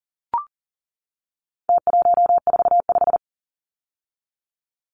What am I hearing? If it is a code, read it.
Morse code, heard as a keyed tone, about 40 wpm, 700 Hz. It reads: T145
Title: Numbers - Contest Style - 4-Digit 40wpm